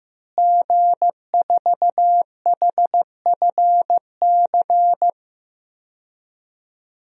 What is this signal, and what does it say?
Morse code, heard as a keyed tone, about 15 wpm, 700 Hz. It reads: G4HFC
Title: DXCC-100 Call Signs - 5 Characters - Repeated 3x - 15wpm